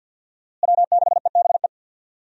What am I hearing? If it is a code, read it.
Morse code, heard as a keyed tone, about 50 wpm, 700 Hz. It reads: W6EBE
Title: USA Call Signs - 5 Characters - Repeated 2x - 50wpm